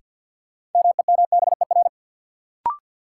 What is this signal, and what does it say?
Morse code, heard as a keyed tone, about 53 wpm, 700 Hz. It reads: MEMBER
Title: Sets of 3 Words - New Word Drill - 50wpm